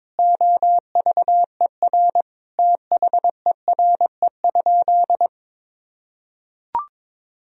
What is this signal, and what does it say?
Morse code, heard as a keyed tone, about 22 wpm, 700 Hz. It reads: OVER THERE?
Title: Sentences from Top 200 Words and Abbreviations 22wpm